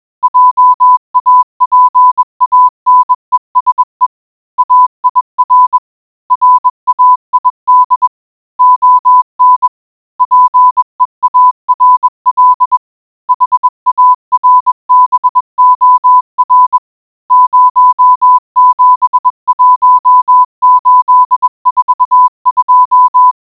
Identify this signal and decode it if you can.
Morse code, heard as a keyed tone, about 21 wpm, 1000 Hz. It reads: JAPANESE AIR RAID ON PEARL HARBOR 071842